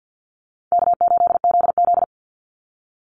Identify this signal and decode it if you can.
Morse code, heard as a keyed tone, about 50 wpm, 700 Hz. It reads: K8ZZ
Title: Easy Call Signs Repeated 3x - 50wpm